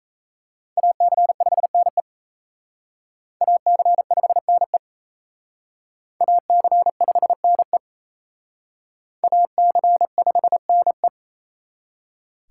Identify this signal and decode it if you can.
Morse code, heard as a keyed tone, about 34 wpm, 700 Hz. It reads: AC5NE AC5NE AC5NE AC5NE
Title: Speed Racing - USA 2x2 Call Signs 28wpm